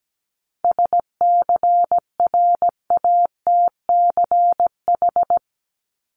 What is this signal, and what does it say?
Morse code, heard as a keyed tone, about 17 wpm, 700 Hz. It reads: SCRATCH
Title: Top 1501-1750 Words 17wpm